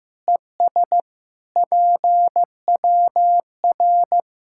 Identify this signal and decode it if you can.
Morse code, heard as a keyed tone, about 16 wpm, 700 Hz. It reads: ES PWR
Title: Practice QSO - Exchange 2 - Info - 15wpm